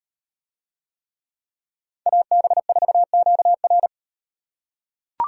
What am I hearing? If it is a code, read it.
Morse code, heard as a keyed tone, about 36 wpm, 700 Hz. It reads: AB4QR